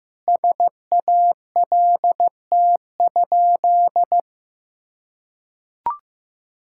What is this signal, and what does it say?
Morse code, heard as a keyed tone, about 15 wpm, 700 Hz. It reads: SALT?
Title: Sentences from Top 1000 Words - Review Part 5 - 15wpm